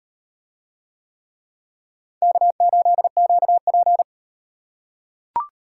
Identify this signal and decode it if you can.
Morse code, heard as a keyed tone, about 35 wpm, 700 Hz. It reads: K8QP